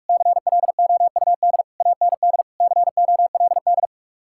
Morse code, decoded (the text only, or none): CLOUD AND COLD